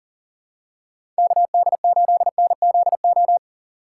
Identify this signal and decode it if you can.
Morse code, heard as a keyed tone, about 40 wpm, 700 Hz. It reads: KD8NZO
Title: USA Call Signs - 6 Characters - 6x Character Spacing - 40wpm